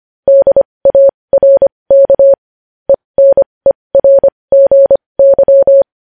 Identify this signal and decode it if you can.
Morse code, heard as a keyed tone, about 25 wpm, spaced as for 20 wpm, 550 Hz. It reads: DARK ENERGY
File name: podcast / News headlines-1x-25x20